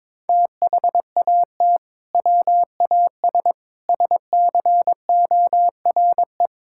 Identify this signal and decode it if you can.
Morse code, heard as a keyed tone, about 22 wpm, 700 Hz. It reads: THAT WAS SCORE
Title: Sentences from Top 1600 Words 22wpm